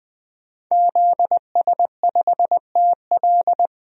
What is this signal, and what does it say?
Morse code, heard as a keyed tone, about 20 wpm, 700 Hz. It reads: ZS5TL
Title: DXCC-100 Call Signs - 5 Characters - 20wpm